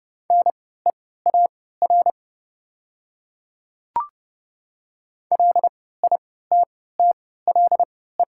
Morse code, heard as a keyed tone, about 30 wpm, spaced as for 17 wpm, 700 Hz. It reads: NEAR LITTLE
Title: 3x Character Spacing - Top 300 Words - 30wpm